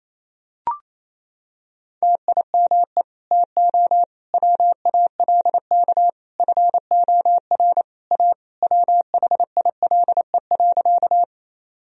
Morse code, heard as a keyed tone, about 28 wpm, 700 Hz. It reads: TIME TO WALK FOR A WHILE.
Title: Sentences from Top 1000 Words - Review Part 1 - 28wpm